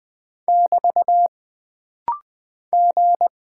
Morse code, heard as a keyed tone, about 20 wpm, 700 Hz. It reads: <BT> G
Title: Single Character (Letter:Number:Punctuation:Prosign) - Rapid-Fire - 20wpm